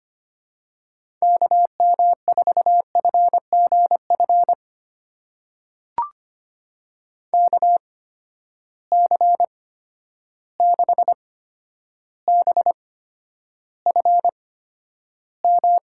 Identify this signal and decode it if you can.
Morse code, heard as a keyed tone, about 25 wpm, spaced as for 11 wpm, 700 Hz. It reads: KM4FGF KC6BFM
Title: USA Call Signs - 6 Characters - 8x Character Spacing - 25wpm